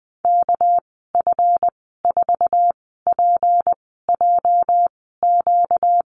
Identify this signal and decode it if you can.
Morse code, heard as a keyed tone, about 19 wpm, spaced as for 14 wpm, 700 Hz. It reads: KF4PJQ
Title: 245 Lesson 39g - Letter z - Call Signs 20wpm